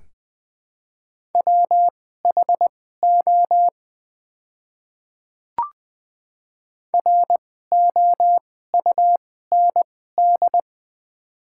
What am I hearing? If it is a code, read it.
Morse code, heard as a keyed tone, about 20 wpm, spaced as for 15 wpm, 700 Hz. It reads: WHO ROUND
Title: 2x Character Spacing - Top 200 Words - 20wpm